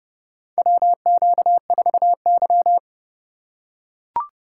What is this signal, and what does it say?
Morse code, heard as a keyed tone, about 30 wpm, 700 Hz. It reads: WQ4Y
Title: USA Call Signs - 4 Characters - 3x Character Spacing - 30wpm